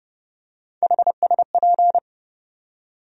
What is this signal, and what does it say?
Morse code, heard as a keyed tone, about 32 wpm, 700 Hz. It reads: HSP
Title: USA Call Sign Postfixes - 3 Letters 30wpm